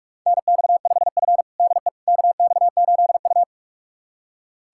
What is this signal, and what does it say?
Morse code, heard as a keyed tone, about 42 wpm, 700 Hz. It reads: NX5F DE KX8U
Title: USA Callsign DE Callsign - 4 Characters - 45wpm